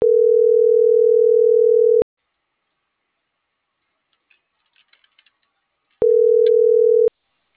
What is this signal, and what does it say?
Ringback tone (North America), 440 + 480 Hz.